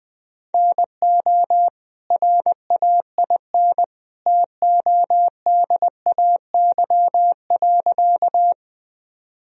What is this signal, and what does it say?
Morse code, heard as a keyed tone, about 20 wpm, 700 Hz. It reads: NO RAIN TODAY.
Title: Sentences from Top 1000 Words - Review Part 1 - 20wpm